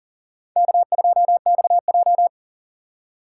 Morse code, heard as a keyed tone, about 49 wpm, 700 Hz. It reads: K2XJ
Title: Speed Racing - USA 1x2 Call Signs 40wpm